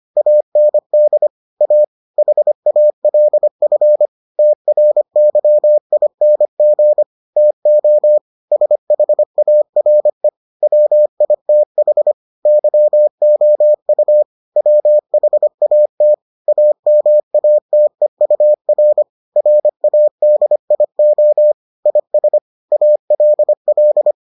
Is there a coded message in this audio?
Yes: Morse code from 0.2 to 24.1 s, heard as a keyed tone, about 25 wpm, 600 Hz. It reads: AND A HALF TRYING TO SHARE WITH YOU WHAT AMATEUR RADIO IS ALL